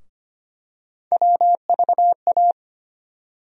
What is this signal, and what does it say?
Morse code, heard as a keyed tone, about 25 wpm, 700 Hz. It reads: WVA
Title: USA Call Sign Postfixes - 3 Letters 25wpm